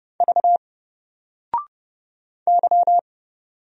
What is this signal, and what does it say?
Morse code, heard as a keyed tone, about 30 wpm, 700 Hz. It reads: V Y
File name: Single-Letter-Number-30wpm